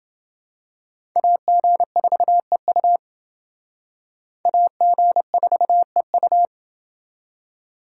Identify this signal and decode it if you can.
Morse code, heard as a keyed tone, about 28 wpm, 700 Hz. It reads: AG4EU AG4EU